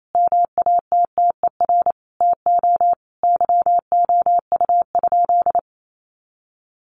Morse code, heard as a keyed tone, about 28 wpm, 700 Hz. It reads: MATTER TO YOU?
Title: Sentences from Top 700 Words 28wpm